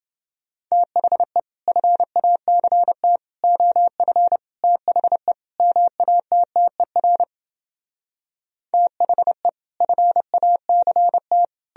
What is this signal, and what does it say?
Morse code, heard as a keyed tone, about 30 wpm, 700 Hz. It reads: THE FACT OF THE MATTER THE FACT